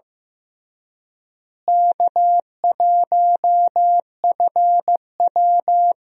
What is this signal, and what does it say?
Morse code, heard as a keyed tone, about 15 wpm, 700 Hz. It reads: K1FW
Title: USA Call Signs - 4 Characters - Repeated 3x - 15wpm